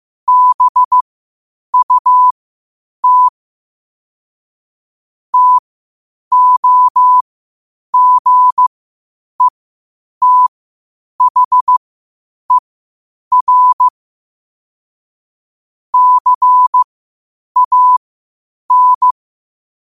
Morse code, heard as a keyed tone, about 15 wpm, spaced as for 8 wpm, 1000 Hz. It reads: BUT TOGETHER CAN
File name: qotdfiles / 8wpm